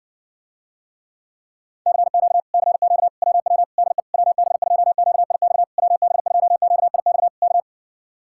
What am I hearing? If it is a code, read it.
Morse code, heard as a keyed tone, about 70 wpm, 700 Hz. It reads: CQ CQ WW DE WB2QIX WB2QIX K